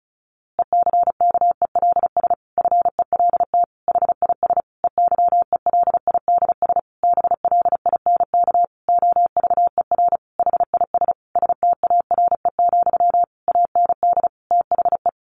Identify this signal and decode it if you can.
Morse code, heard as a keyed tone, about 34 wpm, 700 Hz. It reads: ECKELS FELT HIS EYELIDS BLINK OVER HIS STARE, AND THE
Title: A Sound Of Thunder 35wpm